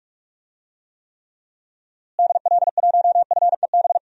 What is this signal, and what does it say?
Morse code, heard as a keyed tone, about 46 wpm, 700 Hz. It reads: DL1FEB
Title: DXCC-100 Call Signs - 6 Characters - Repeated 3x - 45wpm